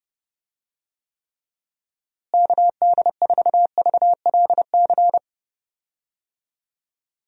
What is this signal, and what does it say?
Morse code, heard as a keyed tone, about 29 wpm, 700 Hz. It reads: KD4VLC
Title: USA Call Signs - 6 Characters - Repeated 2x - 30wpm